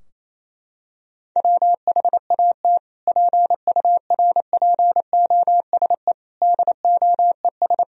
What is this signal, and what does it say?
Morse code, heard as a keyed tone, about 28 wpm, 700 Hz. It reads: WHAT PURPOSE DOES
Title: Sentences from Top 1000 Words - Review Part 2 - 28wpm